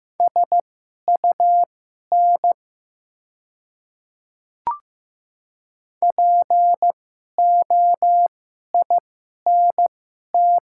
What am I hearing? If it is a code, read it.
Morse code, heard as a keyed tone, about 15 wpm, spaced as for 11 wpm, 700 Hz. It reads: SUN POINT
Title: 2x Character Spacing - Top 300 Words - 15wpm